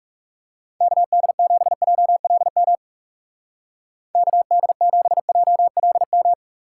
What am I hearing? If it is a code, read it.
Morse code, heard as a keyed tone, about 38 wpm, 700 Hz. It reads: KD7JLM KD7JLM